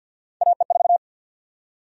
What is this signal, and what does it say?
Morse code, heard as a keyed tone, about 48 wpm, 700 Hz. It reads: AE4
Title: USA Call Sign Prefixes - 50wpm